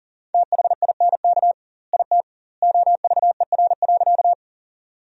Morse code, heard as a keyed tone, about 40 wpm, 700 Hz. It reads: THINK IT OVER.
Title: 2x Word-Spacing - Sentences from Top 200 Words - 40wpm